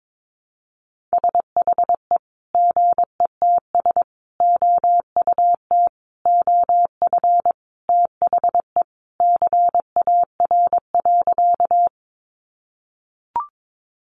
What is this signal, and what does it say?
Morse code, heard as a keyed tone, about 22 wpm, 700 Hz. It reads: SHE GETS OUT OF THE CAR.
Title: Easy Phrases 22wpm